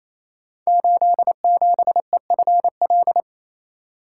Morse code, heard as a keyed tone, about 27 wpm, 700 Hz. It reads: 87EFL
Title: Code Groups - 5 Letter:Numbers - 28wpm